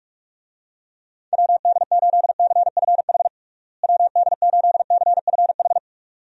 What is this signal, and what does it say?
Morse code, heard as a keyed tone, about 42 wpm, 700 Hz. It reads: WD8CFH WD8CFH
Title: USA Call Signs - 6 Characters - Repeated 2x - 45wpm